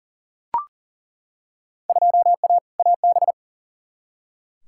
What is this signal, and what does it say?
Morse code, heard as a keyed tone, about 40 wpm, 700 Hz. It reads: 2A AB
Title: Field Day Exchanges - 40wpm